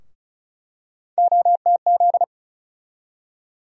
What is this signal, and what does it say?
Morse code, heard as a keyed tone, about 36 wpm, 700 Hz. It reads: OTZ
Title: USA Call Sign Postfixes - 3 Letters 35wpm